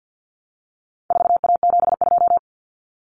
Morse code, heard as a keyed tone, about 50 wpm, 700 Hz. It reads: 4A72
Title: Common Cut Numbers - 4-Digit 50wpm